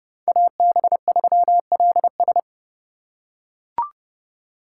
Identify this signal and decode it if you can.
Morse code, heard as a keyed tone, about 29 wpm, 700 Hz. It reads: AB3LS